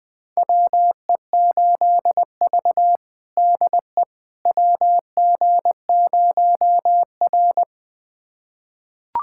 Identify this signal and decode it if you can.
Morse code, heard as a keyed tone, about 21 wpm, 700 Hz. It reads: WE8V DE WG0R
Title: USA Callsign DE Callsign - 4 Characters - 20wpm